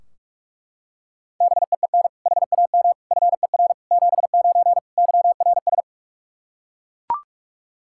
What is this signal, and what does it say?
Morse code, heard as a keyed tone, about 41 wpm, 700 Hz. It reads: BEEN HAM FER 79 YRS
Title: Practice QSO - Exchange 3 - More Info - 45wpm